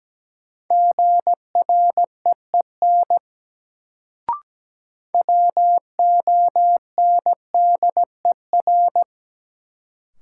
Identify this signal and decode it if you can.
Morse code, heard as a keyed tone, about 17 wpm, 700 Hz. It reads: GREEN WONDER
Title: Sentences from Top 500 Words - New Word Drill - 17wpm